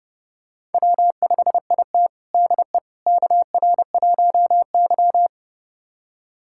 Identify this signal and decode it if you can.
Morse code, heard as a keyed tone, about 30 wpm, 700 Hz. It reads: W5IT DE KR1Y
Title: USA Callsign DE Callsign - 4 Characters - 30wpm